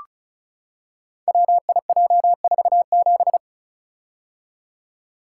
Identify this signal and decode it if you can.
Morse code, heard as a keyed tone, about 36 wpm, 700 Hz. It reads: WIJ47
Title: Code Groups - 5 Letter:Numbers - 35wpm